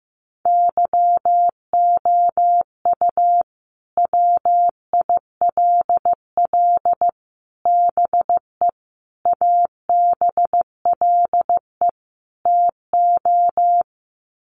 Morse code, heard as a keyed tone, about 15 wpm, 700 Hz. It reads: YOU WILL BE ABLE TO
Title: Sets of 5 Words Encore2 15wpm